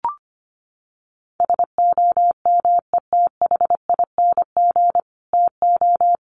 Morse code, heard as a keyed tone, about 25 wpm, 700 Hz. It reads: SOMETHING TO
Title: Sets of 5 Words Encore 25wpm